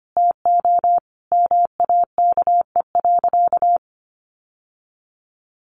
Morse code, heard as a keyed tone, about 25 wpm, 700 Hz. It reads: TO MAKE.
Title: Sentences from Top 1400 Words 25wpm